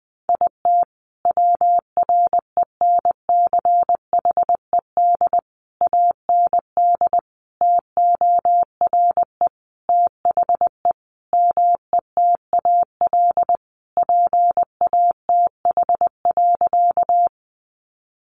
Morse code, heard as a keyed tone, about 20 wpm, 700 Hz. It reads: IT WRENCHED AND TORE THE METAL PATH.
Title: A Sound Of Thunder 20wpm